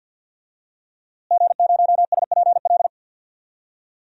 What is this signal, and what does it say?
Morse code, heard as a keyed tone, about 52 wpm, 700 Hz. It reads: G0SPL
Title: DXCC-100 Call Signs - 5 Characters - Repeated 3x - 50wpm